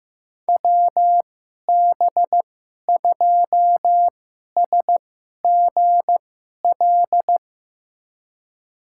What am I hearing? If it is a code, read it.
Morse code, heard as a keyed tone, about 15 wpm, spaced as for 11 wpm, 700 Hz. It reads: WB2SGL
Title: USA Call Signs - 6 Characters - 2x Character Spacing - 15wpm